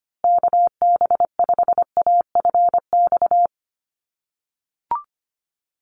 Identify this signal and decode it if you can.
Morse code, heard as a keyed tone, about 25 wpm, 700 Hz. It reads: KB5AFX
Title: USA Call Signs Repeated 3x 25wpm